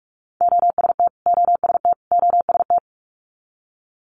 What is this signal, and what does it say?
Morse code, heard as a keyed tone, about 46 wpm, 700 Hz. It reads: OST OST OST
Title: Multi-letter Phonemes Repeated 3x 45wpm